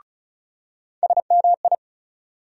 Morse code, heard as a keyed tone, about 35 wpm, 700 Hz. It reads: SMI
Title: USA Call Sign Postfixes - 3 Letters 35wpm